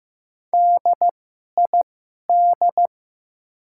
Morse code, heard as a keyed tone, about 15 wpm, spaced as for 11 wpm, 700 Hz. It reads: DID
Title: 2x Character Spacing - Top 100 Words - 15wpm